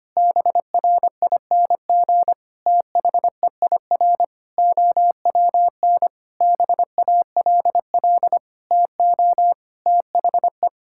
Morse code, heard as a keyed tone, about 25 wpm, 700 Hz. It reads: BRING THEIR OWN BALL TO THE
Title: Sentences from Top 1000 Words - Review Part 6 - 25wpm